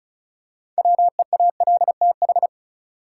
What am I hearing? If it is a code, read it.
Morse code, heard as a keyed tone, about 34 wpm, 700 Hz. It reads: WEALTH
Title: Six Letter Words - 35wpm